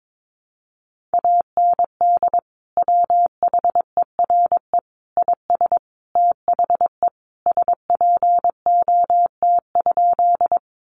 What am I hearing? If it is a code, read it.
Morse code, heard as a keyed tone, about 22 wpm, 700 Hz. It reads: AND WHERE IS THE SPOT?
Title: Sentences from Top 1100 Words 22wpm